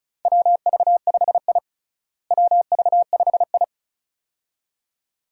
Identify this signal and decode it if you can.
Morse code, heard as a keyed tone, about 36 wpm, 700 Hz. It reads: WV5I WV5I
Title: USA Call Signs - 4 Characters - Repeated 2x - 35wpm